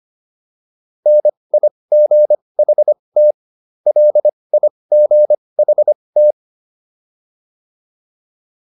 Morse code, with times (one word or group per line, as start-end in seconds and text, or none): 1.1-3.3 NIGHT
3.9-6.3 LIGHT